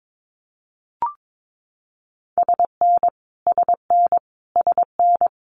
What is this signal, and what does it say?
Morse code, heard as a keyed tone, about 22 wpm, 700 Hz. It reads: SN SN SN
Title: Multi-letter Phonemes Repeated 3x 22wpm